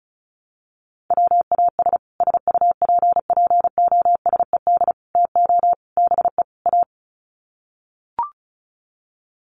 Morse code, heard as a keyed tone, about 34 wpm, 700 Hz. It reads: WAS SUPPOSED TO BE A
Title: Sets of 5 Words Encore 35wpm